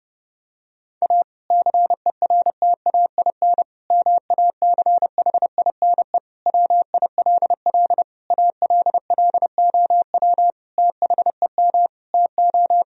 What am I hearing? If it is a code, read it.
Morse code, heard as a keyed tone, about 30 wpm, 700 Hz. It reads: A CERTAIN MACHINE WILL ALLOW THEM TO
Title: Sentences from Top 1000 Words - Review Part 2 - 30wpm